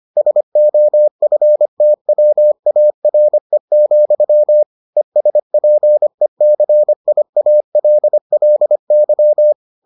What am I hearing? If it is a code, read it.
Morse code, heard as a keyed tone, about 25 wpm, 600 Hz. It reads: SOFTWARE, ESPECIALLY